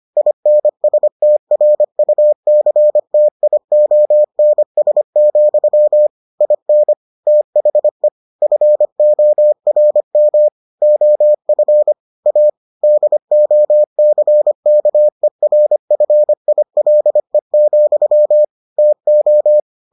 Morse code, heard as a keyed tone, about 25 wpm, 600 Hz. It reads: INSTRUCTIONS, IN THE FORM OF A DOCKERFILE, TO